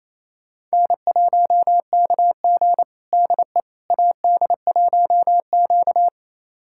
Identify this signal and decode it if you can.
Morse code, heard as a keyed tone, about 28 wpm, 700 Hz. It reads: N1KG DE AD1Q
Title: USA Callsign DE Callsign - 4 Characters - 28wpm